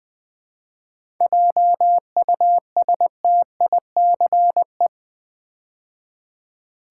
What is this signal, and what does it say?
Morse code, heard as a keyed tone, about 20 wpm, 700 Hz. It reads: JUSTICE
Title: 7 Letter Words - First 3 Letters Given - 20wpm